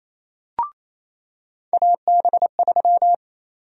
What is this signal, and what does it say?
Morse code, heard as a keyed tone, about 28 wpm, 700 Hz. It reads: AB3
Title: USA Call Sign Prefixes - 28wpm